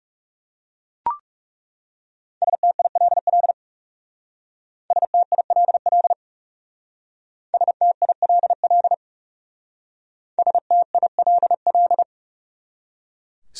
Morse code, heard as a keyed tone, about 37 wpm, 700 Hz. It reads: STILL STILL STILL STILL
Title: Speed-Racing Top 500 Words 30wpm